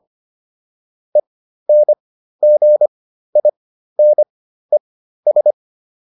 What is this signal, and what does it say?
Morse code, heard as a keyed tone, about 25 wpm, spaced as for 13 wpm, 600 Hz. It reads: ENGINES